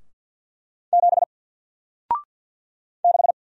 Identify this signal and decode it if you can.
Morse code, heard as a keyed tone, about 49 wpm, 700 Hz. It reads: 7 6
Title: Single Character (Letter:Number:Punctuation:Prosign) Alt 50wpm